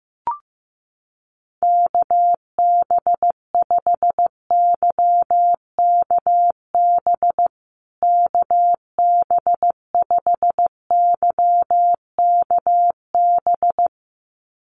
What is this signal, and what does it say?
Morse code, heard as a keyed tone, about 15 wpm, 700 Hz. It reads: KB5YKB KB5YKB
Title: USA Call Signs Repeated 2x 15wpm